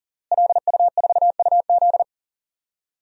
Morse code, heard as a keyed tone, about 40 wpm, 700 Hz. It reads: LU4UZ
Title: DXCC-100 Call Signs - 5 Characters - Repeated 2x - 40wpm